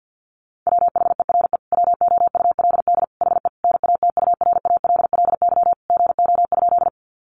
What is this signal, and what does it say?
Morse code, heard as a keyed tone, about 50 wpm, 700 Hz. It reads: WHERE WOULD HE NATURALLY GO?
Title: Sentences from Top 1300 Words 50wpm